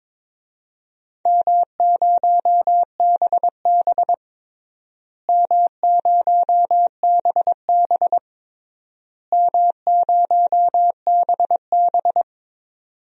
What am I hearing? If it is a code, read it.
Morse code, heard as a keyed tone, about 22 wpm, 700 Hz. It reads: M0BB M0BB M0BB
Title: DXCC-100 Call Signs - 4 Characters - Repeated 3x - 22wpm